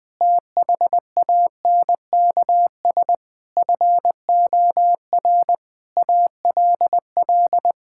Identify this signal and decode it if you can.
Morse code, heard as a keyed tone, about 21 wpm, 700 Hz. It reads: THANKS FOR ALL
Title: Sentences with 8-Word Phrases - 20wpm